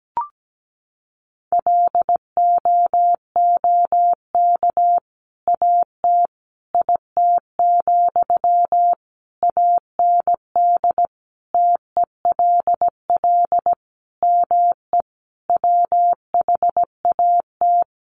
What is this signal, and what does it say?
Morse code, heard as a keyed tone, about 17 wpm, 700 Hz. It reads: LOOK AT IT, AND TELL ME WHAT
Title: Sentences from Top 200 Words 17wpm